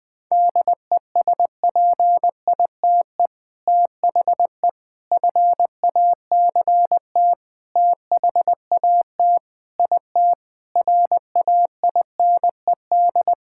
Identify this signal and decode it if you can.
Morse code, heard as a keyed tone, about 20 wpm, 700 Hz. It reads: DESPITE THE FACT THAT IT RAINED
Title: Sentences with 4-Word Phrases - 20wpm